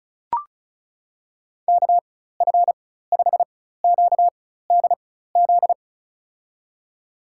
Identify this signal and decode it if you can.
Morse code, heard as a keyed tone, about 36 wpm, spaced as for 16 wpm, 700 Hz. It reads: KF5QDZ
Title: USA Call Signs - 6 Characters - 4x Character Spacing - 35wpm